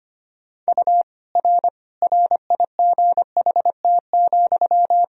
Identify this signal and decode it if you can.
Morse code, heard as a keyed tone, about 25 wpm, 700 Hz. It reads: U R RIGHT,
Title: Sentences from Top 400 Words and Abbreviations 25wpm